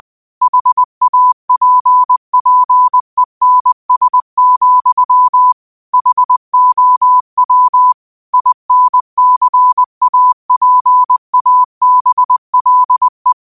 Morse code, heard as a keyed tone, about 20 wpm, 1000 Hz. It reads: HAPPENS, HOW INCAPABLE
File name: qotdfiles / 20wpm